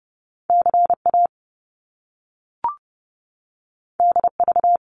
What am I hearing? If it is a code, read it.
Morse code, heard as a keyed tone, about 30 wpm, 700 Hz. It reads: CA DV
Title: Top Two Letter Combinations in Words 30wpm